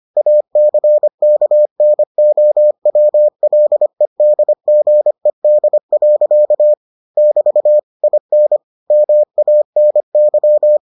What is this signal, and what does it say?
Morse code, heard as a keyed tone, about 25 wpm, 600 Hz. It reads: ACKNOWLEDGED. <BT> IN MANY